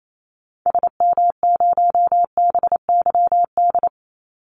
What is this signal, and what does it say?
Morse code, heard as a keyed tone, about 28 wpm, 700 Hz. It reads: SM0BYD
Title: Call Signs - 28wpm